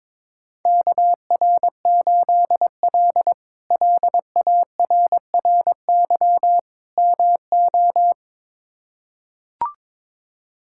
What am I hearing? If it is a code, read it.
Morse code, heard as a keyed tone, about 22 wpm, 700 Hz. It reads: KR8L LARRY MO
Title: K1USN SST QSO Practice - 22wpm